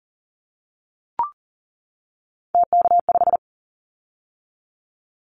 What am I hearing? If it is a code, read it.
Morse code, heard as a keyed tone, about 40 wpm, 700 Hz. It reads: TK5
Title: Callsign Prefixes for Top 100 DXCC - 40wpm